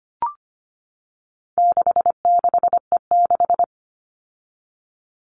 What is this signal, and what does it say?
Morse code, heard as a keyed tone, about 25 wpm, 700 Hz. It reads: 66E6
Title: All Cut Numbers - 4-Digit 25wpm